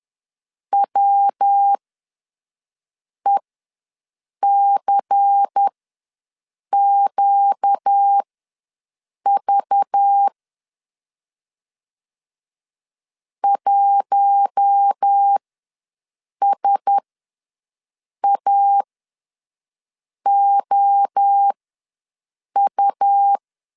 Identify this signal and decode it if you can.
Morse code, heard as a keyed tone, about 10 wpm, spaced as for 5 wpm, 800 Hz. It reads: WECQV 1SAOU
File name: LEZIONI / 5 LBVC / 5 LBVC ws 10wpm 3733